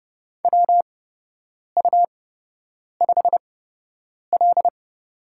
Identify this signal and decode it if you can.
Morse code, heard as a keyed tone, about 30 wpm, spaced as for 8 wpm, 700 Hz. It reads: WU5L
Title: USA Call Signs - 4 Characters - 8x Character Spacing - 30wpm